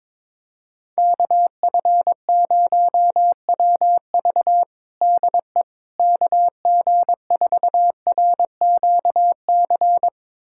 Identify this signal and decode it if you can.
Morse code, heard as a keyed tone, about 22 wpm, 700 Hz. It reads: KF0WV DE KG4RQC
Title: USA Callsign DE Callsign 22wpm